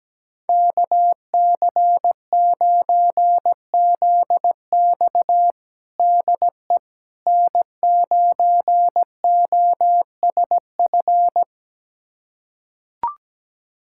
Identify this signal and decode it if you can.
Morse code, heard as a keyed tone, about 17 wpm, 700 Hz. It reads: KC9ZX DE N9OSF
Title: USA Callsign DE Callsign - 5 Characters - 17wpm